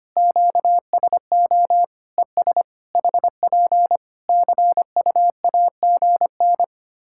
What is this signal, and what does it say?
Morse code, heard as a keyed tone, about 25 wpm, 700 Hz. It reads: QSO ES HP CUAGN
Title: Practice QSO - Exchange 4 - Ending - 25wpm